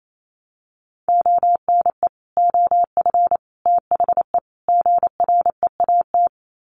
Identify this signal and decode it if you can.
Morse code, heard as a keyed tone, about 28 wpm, 700 Hz. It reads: ONE OF THE GREAT
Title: Sets of 4 Words 28wpm